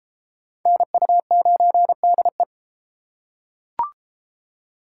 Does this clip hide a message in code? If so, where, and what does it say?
Yes: Morse code from 0.7 to 2.4 s, heard as a keyed tone, about 32 wpm, 700 Hz. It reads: NU9DE